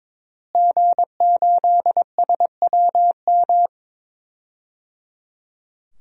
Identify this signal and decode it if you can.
Morse code, heard as a keyed tone, about 22 wpm, 700 Hz. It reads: G8SWM
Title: DXCC-100 Call Signs - 5 Characters - Repeated 3x - 22wpm